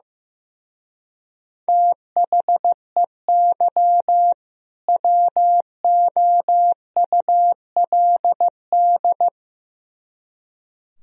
Morse code, heard as a keyed tone, about 15 wpm, 700 Hz. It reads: THEY WOULD
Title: Speed-Racing - Sets of 2 Words 15wpm